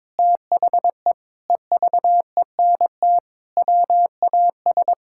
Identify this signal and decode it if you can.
Morse code, heard as a keyed tone, about 22 wpm, 700 Hz. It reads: THE EVENT WAS
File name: practice / 22wpm / Sentences with 8-Word Phrases - 22wpm